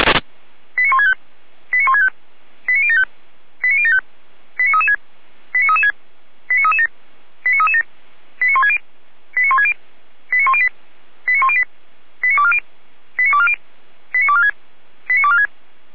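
Five-tone selective call (ZVEI), heard as quick runs of five tones, tones of about 70 ms. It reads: 89165 89165 89075 89075 89336 89336 89337 89337 89160 89160 89108 89108 89240 89240 89246 89246
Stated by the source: ZVEI five-tone selective call received by SDR.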